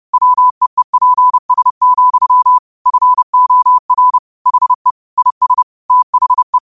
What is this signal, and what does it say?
Morse code, heard as a keyed tone, about 30 wpm, 1000 Hz. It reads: WEEPS, FOR HE IS THE